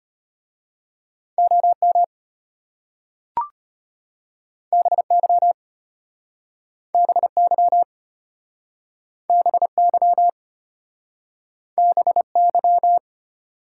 Morse code, heard as a keyed tone, about 33 wpm, 700 Hz. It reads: OM BY BY BY BY